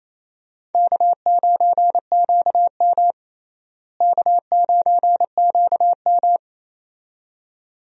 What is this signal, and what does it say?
Morse code, heard as a keyed tone, about 27 wpm, 700 Hz. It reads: K9QM K9QM
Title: USA Call Signs - 4 Characters - Repeated 2x - 28wpm